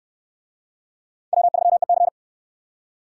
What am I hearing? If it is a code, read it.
Morse code, heard as a keyed tone, about 67 wpm, 700 Hz. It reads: W3EC